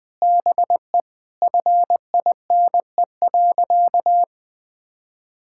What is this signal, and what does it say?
Morse code, heard as a keyed tone, about 20 wpm, 700 Hz. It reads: BE FINE.
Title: Sentences with 7-Word Phrases - Spoken Beginning - 20wpm